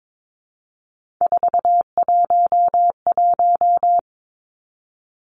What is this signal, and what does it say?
Morse code, heard as a keyed tone, about 22 wpm, 700 Hz. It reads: 411
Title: Three Challenging Letter-Number Combinations 22wpm